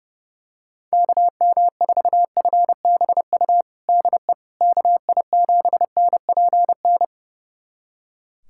Morse code, heard as a keyed tone, about 32 wpm, 700 Hz. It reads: KM4FBU DE KI7NPN
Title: USA Callsign DE Callsign 30wpm